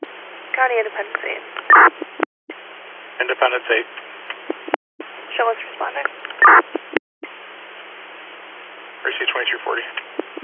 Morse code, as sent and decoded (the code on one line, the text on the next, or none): none